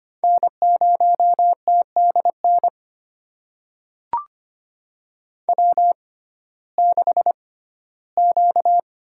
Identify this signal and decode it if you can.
Morse code, heard as a keyed tone, about 25 wpm, spaced as for 15 wpm, 700 Hz. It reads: N0TDN W6Q
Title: USA Call Signs - 5 Characters - 6x Character Spacing - 25wpm